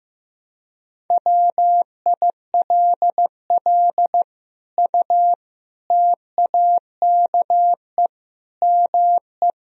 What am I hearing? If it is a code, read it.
Morse code, heard as a keyed tone, about 16 wpm, 700 Hz. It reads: WILL U TAKE ME
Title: Sentences from Top 200 Words and Abbreviations 15wpm